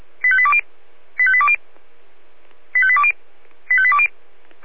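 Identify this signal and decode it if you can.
Five-tone selective call (ZVEI), heard as quick runs of five tones, tones of about 70 ms. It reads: 85820 85820 85820 85820
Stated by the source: Five-tone ZVEI selective call 85217, received on SDR.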